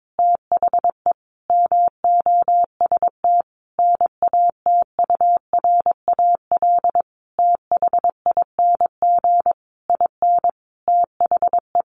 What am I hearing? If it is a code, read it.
Morse code, heard as a keyed tone, about 22 wpm, 700 Hz. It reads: THE MOST NATURAL THING IN THE
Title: Sentences from Top 700 Words 22wpm